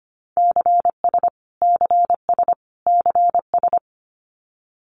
Morse code, heard as a keyed tone, about 25 wpm, 700 Hz. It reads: CS CS CS
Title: Abbreviations Repeated 3x - 25wpm